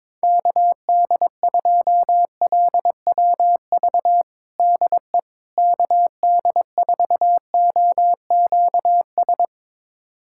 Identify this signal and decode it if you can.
Morse code, heard as a keyed tone, about 22 wpm, 700 Hz. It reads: KD2LWV DE KD4OQS
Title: USA Callsign DE Callsign - 6 Characters - 22wpm